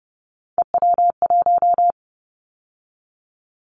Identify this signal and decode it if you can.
Morse code, heard as a keyed tone, about 30 wpm, 700 Hz. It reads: EW1
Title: Callsign Prefixes for Top 100 DXCC - 30wpm